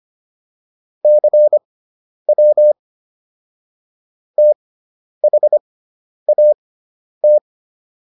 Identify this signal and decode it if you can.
Morse code, heard as a keyed tone, about 25 wpm, spaced as for 10 wpm, 600 Hz. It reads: CW THAT